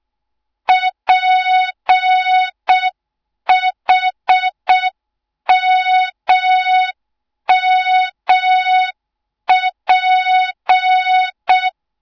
Morse code, heard as a keyed tone, about 6 wpm, 3700 Hz. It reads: PHMMP